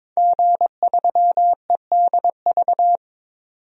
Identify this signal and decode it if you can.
Morse code, heard as a keyed tone, about 22 wpm, 700 Hz. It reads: G3EDV
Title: DXCC-100 Call Signs - 5 Characters - 22wpm